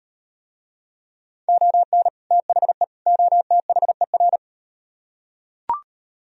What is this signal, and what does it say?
Morse code, heard as a keyed tone, about 35 wpm, 700 Hz. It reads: ON THE OTHER